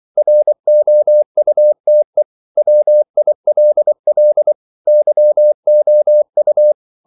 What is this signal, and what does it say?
Morse code, heard as a keyed tone, about 24 wpm, 600 Hz. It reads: ROUTE WILL YOU